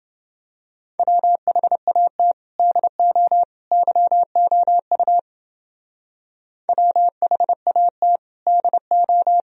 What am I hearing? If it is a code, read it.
Morse code, heard as a keyed tone, about 29 wpm, 700 Hz. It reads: WHAT DO YOU WHAT DO